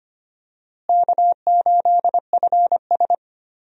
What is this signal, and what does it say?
Morse code, heard as a keyed tone, about 25 wpm, 700 Hz. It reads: K8FS
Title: USA Call Signs - 4 Characters - Repeated 3x - 25wpm